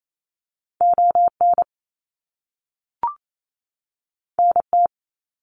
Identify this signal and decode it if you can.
Morse code, heard as a keyed tone, about 28 wpm, 700 Hz. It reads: ON NT
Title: Canadian Provinces and Territories 28wpm